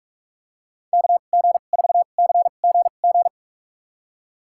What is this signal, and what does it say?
Morse code, heard as a keyed tone, about 45 wpm, spaced as for 32 wpm, 700 Hz. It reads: KG4CGG
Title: USA Call Signs - 6 Characters - 2x Character Spacing - 45wpm